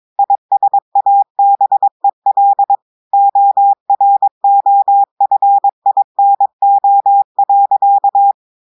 Morse code, heard as a keyed tone, about 22 wpm, 800 Hz. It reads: ISABEL OROFINO.
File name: audios / portugais / portugais texte en clair 1 vitesse 22 mots minutes